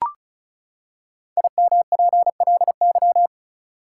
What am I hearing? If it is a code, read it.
Morse code, heard as a keyed tone, about 36 wpm, 700 Hz. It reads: IMPLY
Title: Five Letter Words - 35wpm